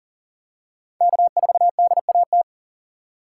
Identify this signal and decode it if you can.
Morse code, heard as a keyed tone, about 40 wpm, 700 Hz. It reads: K4DAT
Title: USA Call Signs - 4x Character Spacing - 40wpm